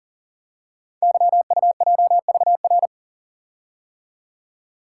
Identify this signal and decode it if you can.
Morse code, heard as a keyed tone, about 38 wpm, 700 Hz. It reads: YUJVR
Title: Code Groups - 5 Letters - 40wpm